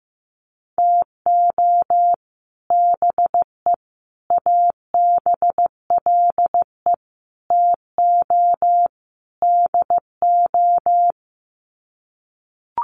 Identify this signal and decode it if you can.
Morse code, heard as a keyed tone, about 15 wpm, 700 Hz. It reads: TO BE ABLE TO DO
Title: Sets of 5 Words 15wpm